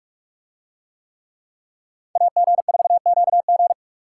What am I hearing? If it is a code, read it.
Morse code, heard as a keyed tone, about 46 wpm, 700 Hz. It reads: AG4QG